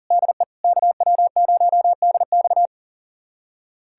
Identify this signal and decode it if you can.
Morse code, heard as a keyed tone, about 40 wpm, 700 Hz. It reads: DE KW0DX
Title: USA Callsign DE Callsign - 5 Characters - 40wpm